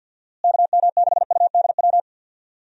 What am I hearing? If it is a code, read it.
Morse code, heard as a keyed tone, about 49 wpm, 700 Hz. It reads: KM6UDW
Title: USA Call Signs - 6 Characters - Repeated 3x - 50wpm